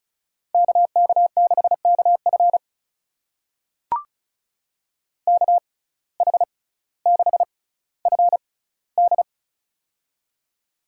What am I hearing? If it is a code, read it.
Morse code, heard as a keyed tone, about 34 wpm, spaced as for 18 wpm, 700 Hz. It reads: KK6KF KH6FD
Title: USA Call Signs - 6x Character Spacing - 35wpm